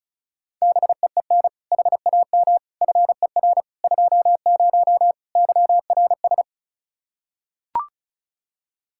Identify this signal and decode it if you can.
Morse code, heard as a keyed tone, about 36 wpm, 700 Hz. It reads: BEEN HAM FER 20 YRS
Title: Practice QSO - Exchange 3 - More Info - 35wpm